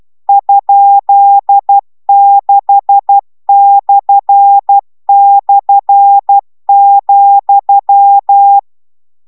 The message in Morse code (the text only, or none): ?6//,